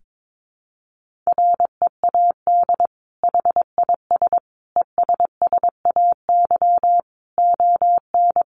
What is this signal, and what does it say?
Morse code, heard as a keyed tone, about 22 wpm, 700 Hz. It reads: READ HIS ESSAY ON
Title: Sentences from Top 1500 Words 22wpm